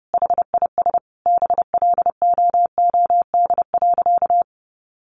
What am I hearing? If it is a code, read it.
Morse code, heard as a keyed tone, about 30 wpm, 700 Hz. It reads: HIS BLOOD.
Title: Sentences from Top 1400 Words 30wpm